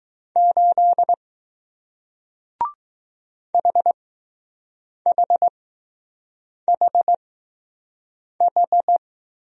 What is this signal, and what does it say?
Morse code, heard as a keyed tone, about 20 wpm, spaced as for 6 wpm, 700 Hz. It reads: 8 HHHH